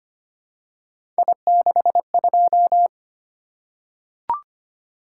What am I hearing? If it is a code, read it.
Morse code, heard as a keyed tone, about 25 wpm, 700 Hz. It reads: I62
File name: Three Letter-Number Combinations - 25wpm